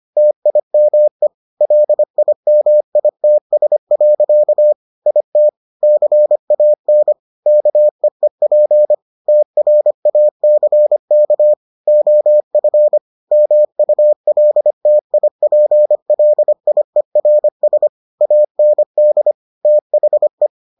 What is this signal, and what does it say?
Morse code, heard as a keyed tone, about 25 wpm, 600 Hz. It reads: TIME LIMITS. IT CAN KEEP TRACK OF MULTIPLIERS AND THE